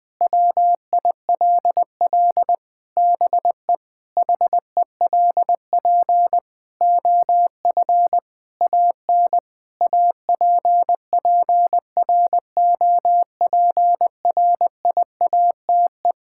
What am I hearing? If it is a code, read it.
Morse code, heard as a keyed tone, about 20 wpm, 700 Hz. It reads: WILL BE HELP OF AN APPROPRIATE
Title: Sentences from Top 1000 Words - Review Part 5 - 20wpm